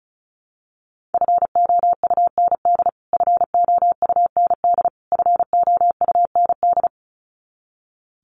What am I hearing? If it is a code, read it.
Morse code, heard as a keyed tone, about 34 wpm, 700 Hz. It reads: FOUND FOUND FOUND
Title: Top 500 Words Repeated 35wpm